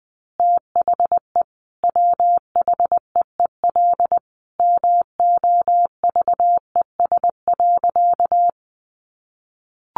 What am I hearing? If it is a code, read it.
Morse code, heard as a keyed tone, about 20 wpm, 700 Hz. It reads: THE WHEEL MOVES.
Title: Additional Easy Phrases 20wpm